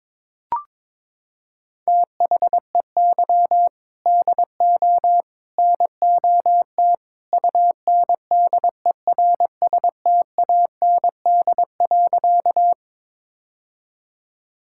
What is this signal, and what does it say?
Morse code, heard as a keyed tone, about 23 wpm, 700 Hz. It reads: THEY DO NOT UNDERSTAND.
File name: practice / 22wpm / Sentences with 3-Word Phrases - 22wpm